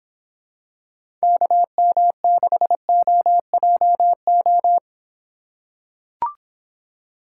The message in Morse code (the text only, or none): KM6OJO